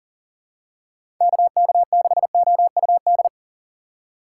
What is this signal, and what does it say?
Morse code, heard as a keyed tone, about 40 wpm, 700 Hz. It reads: KK6OUD
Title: USA Call Signs - 6x Character Spacing - 40wpm